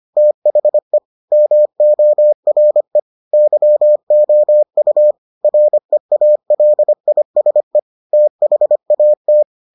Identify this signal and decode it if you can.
Morse code, heard as a keyed tone, about 25 wpm, 600 Hz. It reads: THE MORE YOU REALISE THAT